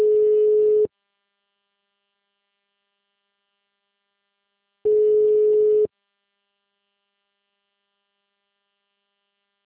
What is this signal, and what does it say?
Ringback tone (Europe, 425 Hz).